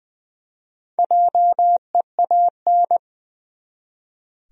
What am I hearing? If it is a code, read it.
Morse code, heard as a keyed tone, about 20 wpm, 700 Hz. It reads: JEAN
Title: USA Common Names - Max 5 Letters - 20wpm